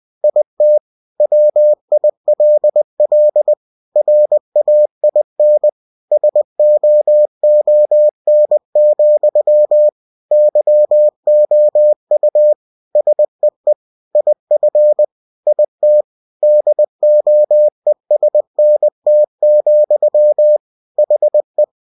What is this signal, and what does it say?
Morse code, heard as a keyed tone, about 20 wpm, 600 Hz. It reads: IT WILL RAIN SOON, YOU SEE IF IT DOESNT, HE
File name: cw-books / winnie-the-pooh / winnie-20wpm-page-0106